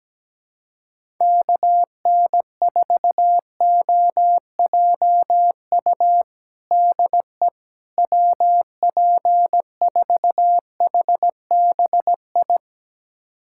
Morse code, heard as a keyed tone, about 17 wpm, 700 Hz. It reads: KN4OJU DE WP4HBI
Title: USA Callsign DE Callsign 17wpm